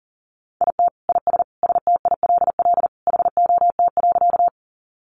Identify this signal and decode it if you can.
Morse code, heard as a keyed tone, about 40 wpm, 700 Hz. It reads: IT IS STILL HOT.
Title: Additional Easy Phrases 40wpm